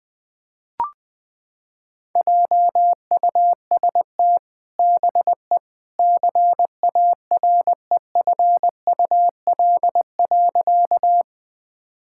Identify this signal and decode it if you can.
Morse code, heard as a keyed tone, about 20 wpm, 700 Hz. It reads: JUST BE CAREFUL.
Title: Sentences from Top 1600 Words 20wpm